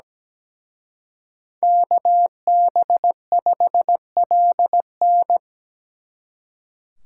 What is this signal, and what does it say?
Morse code, heard as a keyed tone, about 18 wpm, 700 Hz. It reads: KB5LN
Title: Speed Racing - USA 2x2 Call Signs 17wpm